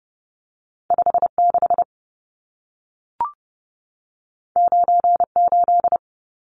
Morse code, heard as a keyed tone, about 30 wpm, 700 Hz. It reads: 56 98
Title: Numbers - 2-Digit 30wpm